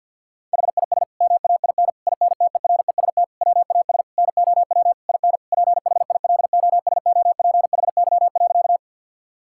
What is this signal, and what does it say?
Morse code, heard as a keyed tone, about 47 wpm, 700 Hz. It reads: HIS MAIN INTEREST WAS NOW IN PHILOSOPHY.
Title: Sentences from Top 1000 Words - Review Part 6 - 50wpm